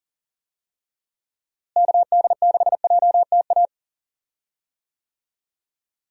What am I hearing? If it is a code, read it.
Morse code, heard as a keyed tone, about 40 wpm, 700 Hz. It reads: KD6JTA
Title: USA Call Signs - 6 Characters - Repeated 3x - 40wpm